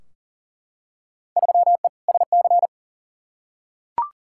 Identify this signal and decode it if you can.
Morse code, heard as a keyed tone, about 40 wpm, 700 Hz. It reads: 3E SC
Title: Field Day Exchanges Encore - 40wpm